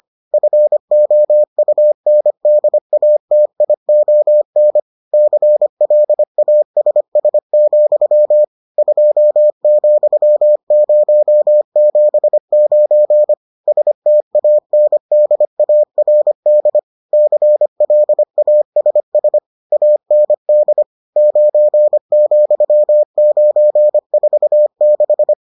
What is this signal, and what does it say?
Morse code, heard as a keyed tone, about 25 wpm, 600 Hz. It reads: FOUNDATION CLASS, 2,079 STANDARD CLASS AND 9,946